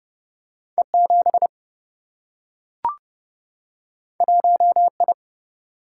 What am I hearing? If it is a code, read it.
Morse code, heard as a keyed tone, about 30 wpm, 700 Hz. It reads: E7 1I
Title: Single-Letter Single-Number 30wpm